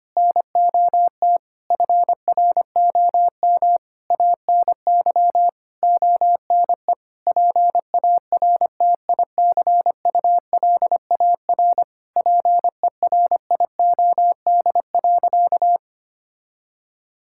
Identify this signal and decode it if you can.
Morse code, heard as a keyed tone, about 25 wpm, 700 Hz. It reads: NOT FROM ANY ONE PARTICULAR PERIOD.
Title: Sentences from Top 1000 Words - Review Part 5 - 25wpm